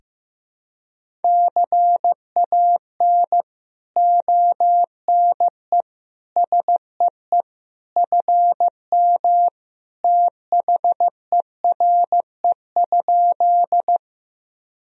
Morse code, heard as a keyed tone, about 15 wpm, 700 Hz. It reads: CAN ONE SEE FM THERE?
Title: Sentences from Top 100 Words and Abbreviations 15wpm